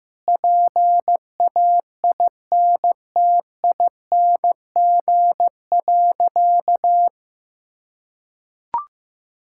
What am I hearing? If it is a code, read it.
Morse code, heard as a keyed tone, about 16 wpm, 700 Hz. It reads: PAINTING.
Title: Sentences from Top 1000 Words - Review Part 6 - 15wpm